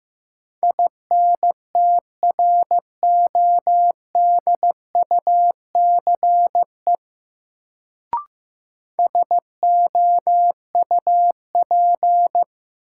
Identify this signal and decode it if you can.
Morse code, heard as a keyed tone, about 15 wpm, 700 Hz. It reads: INTRODUCE SOUP
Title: Sentences from Top 1600 Words - New Word Drill Alt - 15wpm